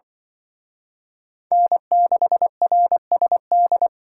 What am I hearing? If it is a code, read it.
Morse code, heard as a keyed tone, about 23 wpm, 700 Hz. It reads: N6RSD
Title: Speed Racing - USA 1x3 Call Signs 20wpm